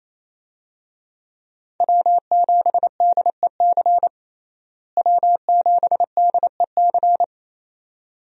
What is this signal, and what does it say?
Morse code, heard as a keyed tone, about 27 wpm, 700 Hz. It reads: W7DEC W7DEC
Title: USA Call Signs - 5 Characters - Repeated 2x - 28wpm